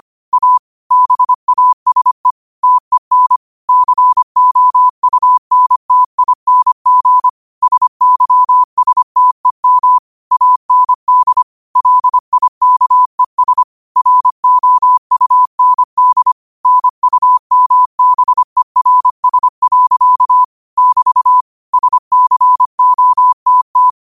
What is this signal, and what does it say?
Morse code, heard as a keyed tone, about 25 wpm, 1000 Hz. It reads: A BASE TEN COUNTING SYSTEM AND LIKES ROUND NUMBERS. <BT> SCOTT